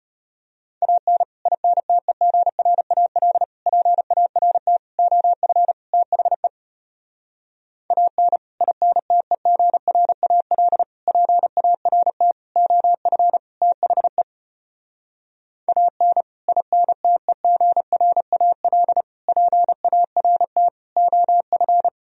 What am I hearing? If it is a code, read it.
Morse code, heard as a keyed tone, about 33 wpm, 700 Hz. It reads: AN INTEGRAL PART OF THE AN INTEGRAL PART OF THE AN INTEGRAL PART OF